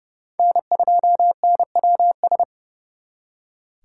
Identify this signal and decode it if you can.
Morse code, heard as a keyed tone, about 36 wpm, 700 Hz. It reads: N2NWS
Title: Speed Racing - USA 1x3 Call Signs 30wpm